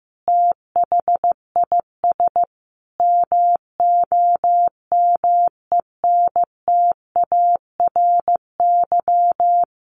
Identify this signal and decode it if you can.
Morse code, heard as a keyed tone, about 15 wpm, 700 Hz. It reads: THIS MOMENTARY
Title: A Sound Of Thunder 15wpm